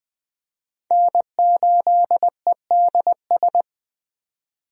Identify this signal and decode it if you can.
Morse code, heard as a keyed tone, about 21 wpm, 700 Hz. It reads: N8EDS
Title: USA Call Signs - 5 Characters - 20wpm